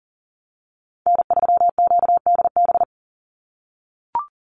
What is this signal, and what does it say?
Morse code, heard as a keyed tone, about 40 wpm, 700 Hz. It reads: N3QDB
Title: USA Call Signs 40wpm